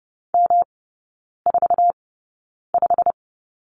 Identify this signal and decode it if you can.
Morse code, heard as a keyed tone, about 30 wpm, spaced as for 9 wpm, 700 Hz. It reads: M45
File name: Lesson 18e - Letter m - Three Characters 30wpm